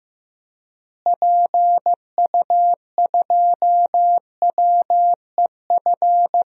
Morse code, heard as a keyed tone, about 15 wpm, 700 Hz. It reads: PU2WEF
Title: DXCC-100 Call Signs - 6 Characters - Repeated 3x - 15wpm